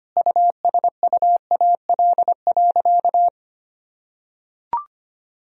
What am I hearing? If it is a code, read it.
Morse code, heard as a keyed tone, about 25 wpm, 700 Hz. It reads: USUAL.
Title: Sentences from Top 1000 Words - Review Part 2 - 25wpm